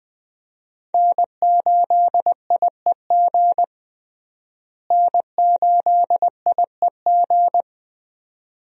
Morse code, heard as a keyed tone, about 21 wpm, 700 Hz. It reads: N8IEG N8IEG
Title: USA Call Signs - 5 Characters - Repeated 2x - 20wpm